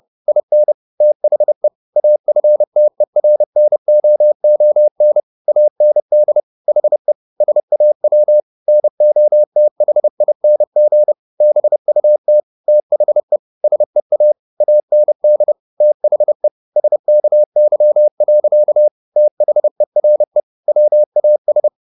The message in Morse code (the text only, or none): IN THE AFTERNOON AND HE SAW NOTHING BUT THE SEA AND THE SKY. THERE WAS